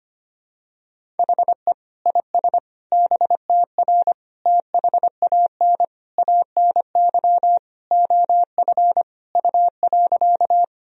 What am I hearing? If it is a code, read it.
Morse code, heard as a keyed tone, about 25 wpm, 700 Hz. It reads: HE IS BTR THAN ANY OF U.
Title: Sentences from Top 400 Words and Abbreviations 25wpm